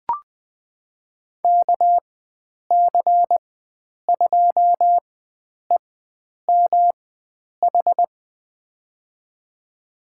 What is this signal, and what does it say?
Morse code, heard as a keyed tone, about 20 wpm, spaced as for 9 wpm, 700 Hz. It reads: KC2EMH
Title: USA Call Signs - 6 Characters - 4x Character Spacing - 20wpm